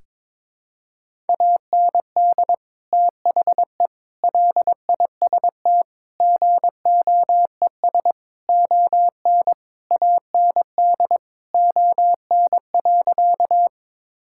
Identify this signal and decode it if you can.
Morse code, heard as a keyed tone, about 22 wpm, 700 Hz. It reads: AND THE LIST GOES ON AND ON.
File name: Sentences with 7-Word Phrases - 22wpm